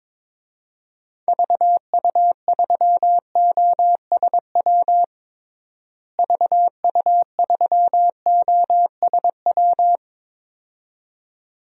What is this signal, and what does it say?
Morse code, heard as a keyed tone, about 22 wpm, 700 Hz. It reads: VU3OSW VU3OSW
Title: DXCC-100 Call Signs - 6 Characters - Repeated 2x - 22wpm